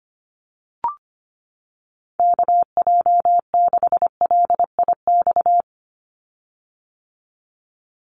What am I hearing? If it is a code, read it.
Morse code, heard as a keyed tone, about 25 wpm, 690 Hz. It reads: KJ6LIX
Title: USA Call Signs 25wpm